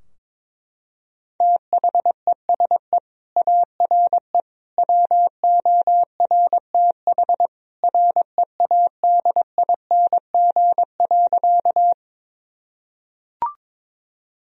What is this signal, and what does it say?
Morse code, heard as a keyed tone, about 22 wpm, 700 Hz. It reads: THESE ARE WORTH READING.
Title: Sentences from Top 1000 Words - Review Part 3 - 22wpm